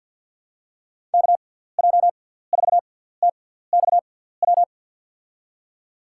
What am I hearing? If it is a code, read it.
Morse code, heard as a keyed tone, about 52 wpm, spaced as for 17 wpm, 690 Hz. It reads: KJ4TXW
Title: USA Call Signs - 6x Character Spacing - 50wpm